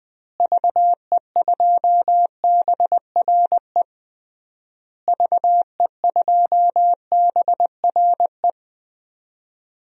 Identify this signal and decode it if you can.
Morse code, heard as a keyed tone, about 20 wpm, 700 Hz. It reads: VE2BRE VE2BRE
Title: DXCC-100 Call Signs - 6 Characters - Repeated 3x - 20wpm